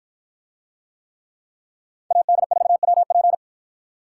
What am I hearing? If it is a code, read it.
Morse code, heard as a keyed tone, about 51 wpm, 700 Hz. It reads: AD4WP